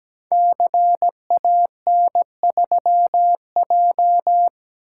Morse code, heard as a keyed tone, about 17 wpm, 700 Hz. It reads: CAN3J
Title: Code Groups - 5 Letter:Numbers - 17wpm